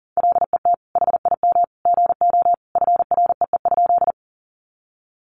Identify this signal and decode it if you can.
Morse code, heard as a keyed tone, about 37 wpm, 700 Hz. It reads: LET HIM GO FREE?
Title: Sentences from Top 500 Words 40wpm